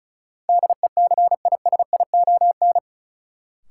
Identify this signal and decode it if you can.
Morse code, heard as a keyed tone, about 36 wpm, 700 Hz. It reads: DECISION
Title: Sentences from Top 1000 Words - New Word Drill - 35wpm